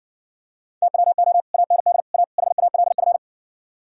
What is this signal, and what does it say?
Morse code, heard as a keyed tone, about 54 wpm, 700 Hz. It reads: TWO AND A HALF